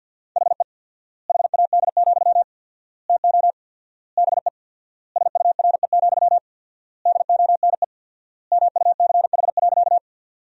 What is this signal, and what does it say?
Morse code, heard as a keyed tone, about 48 wpm, 700 Hz. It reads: HE HAD, TO BE SURE, DONE MUCH.
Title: 4x Word-Spacing - Sentences from Top 500 Words - 50wpm